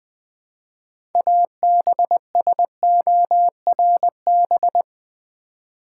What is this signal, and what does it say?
Morse code, heard as a keyed tone, about 20 wpm, 700 Hz. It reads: ABSORB
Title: 6 Letter Words - First 3 Letters Given - 20wpm